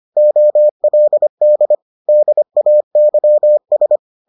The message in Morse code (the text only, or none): OLD DAYS